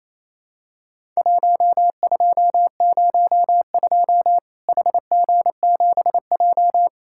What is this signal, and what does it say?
Morse code, heard as a keyed tone, about 32 wpm, 700 Hz. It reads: 1202 HG7J
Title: Worked All Europe DX Contest - QTC Exchange - 28wpm